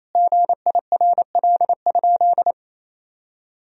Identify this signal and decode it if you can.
Morse code, heard as a keyed tone, about 28 wpm, 700 Hz. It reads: GIRL?
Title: Sentences from Top 1000 Words - Review Part 1 - 28wpm